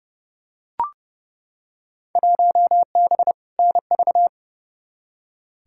Morse code, heard as a keyed tone, about 32 wpm, 700 Hz. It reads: 1B NV
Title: Field Day Exchanges - 30wpm